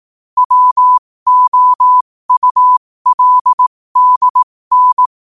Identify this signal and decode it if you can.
Morse code, heard as a keyed tone, about 18 wpm, spaced as for 15 wpm, 1000 Hz. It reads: WOULDN